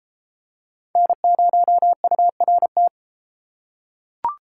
Morse code, heard as a keyed tone, about 33 wpm, 700 Hz. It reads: N0URT